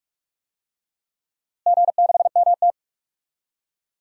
Morse code, heard as a keyed tone, about 45 wpm, 700 Hz. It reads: G6MT
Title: DXCC-100 Call Signs - 4 Characters - Repeated 2x - 45wpm